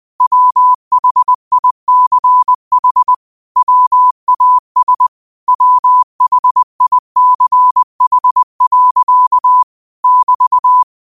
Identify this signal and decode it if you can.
Morse code, heard as a keyed tone, about 20 wpm, 1000 Hz. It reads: WHICH WAS WHICH. <BT>